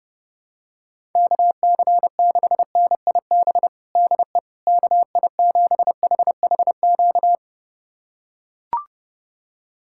Morse code, heard as a keyed tone, about 31 wpm, 700 Hz. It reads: KC6NIB DE KI7HHQ
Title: USA Callsign DE Callsign - 6 Characters - 30wpm